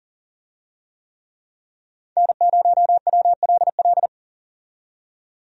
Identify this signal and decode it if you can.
Morse code, heard as a keyed tone, about 40 wpm, 700 Hz. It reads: N0WLL
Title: USA Call Signs - 5 Characters - 40wpm